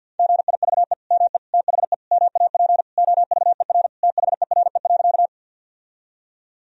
Morse code, heard as a keyed tone, about 50 wpm, 700 Hz. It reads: GIVE ME THE MAP OVER THERE.
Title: Sentences from Top 1000 Words - Review Part 2 - 50wpm